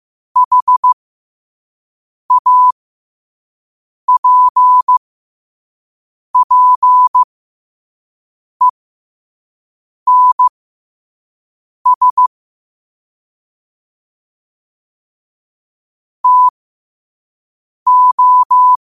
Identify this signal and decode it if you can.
Morse code, heard as a keyed tone, about 15 wpm, spaced as for 5 wpm, 1000 Hz. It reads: HAPPENS TO
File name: qotdfiles / 5wpm